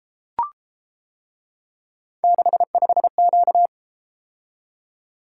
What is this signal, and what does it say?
Morse code, heard as a keyed tone, about 30 wpm, 700 Hz. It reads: 65Q